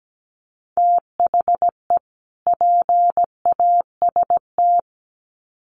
Morse code, heard as a keyed tone, about 17 wpm, 700 Hz. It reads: THE PAST
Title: Sets of 2 Words 17wpm